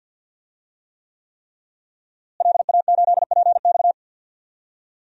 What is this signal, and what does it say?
Morse code, heard as a keyed tone, about 52 wpm, 700 Hz. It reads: LA8PX
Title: DXCC-100 Call Signs - 5 Characters - Repeated 3x - 50wpm